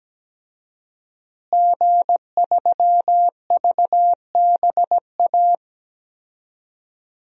Morse code, heard as a keyed tone, about 17 wpm, 700 Hz. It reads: G3VBA
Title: DXCC-100 Call Signs - 5 Characters - Repeated 3x - 17wpm